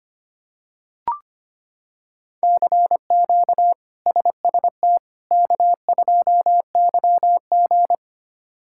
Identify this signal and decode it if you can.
Morse code, heard as a keyed tone, about 25 wpm, 700 Hz. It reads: CQ SST K2YG
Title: K1USN SST QSO Practice - 25wpm